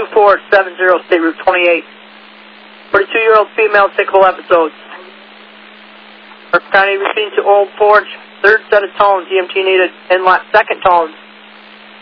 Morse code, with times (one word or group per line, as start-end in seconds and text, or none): none